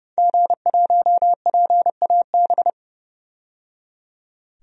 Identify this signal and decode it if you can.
Morse code, heard as a keyed tone, about 31 wpm, 700 Hz. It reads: G1PAB
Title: DXCC-100 Call Signs - 5 Characters - 30wpm